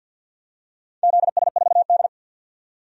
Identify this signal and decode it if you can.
Morse code, heard as a keyed tone, about 51 wpm, 700 Hz. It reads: ZS4D
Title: DXCC-100 Call Signs - 4 Characters - 50wpm